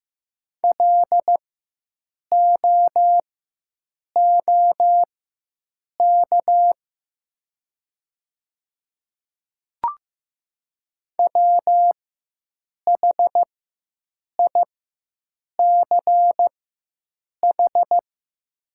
Morse code, heard as a keyed tone, about 15 wpm, spaced as for 7 wpm, 700 Hz. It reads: LOOK WHICH
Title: 4x Character Spacing - Top 100 Words - 15wpm